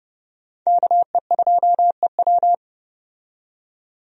Morse code, heard as a keyed tone, about 30 wpm, 700 Hz. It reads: KE2EW
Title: USA Call Signs - 5 Characters - 30wpm